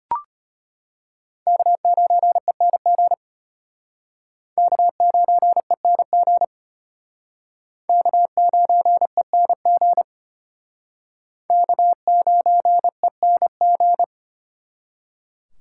Morse code, heard as a keyed tone, about 32 wpm, 700 Hz. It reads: K9ENG K9ENG K9ENG K9ENG